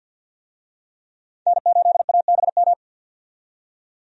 Morse code, heard as a keyed tone, about 50 wpm, 700 Hz. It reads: N8ABM
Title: USA Call Signs - 5 Characters - 50wpm